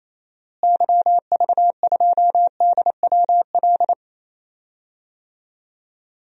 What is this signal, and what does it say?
Morse code, heard as a keyed tone, about 28 wpm, 700 Hz. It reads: YV2DWL
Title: DXCC-100 Call Signs - 6 Characters - Repeated 3x - 28wpm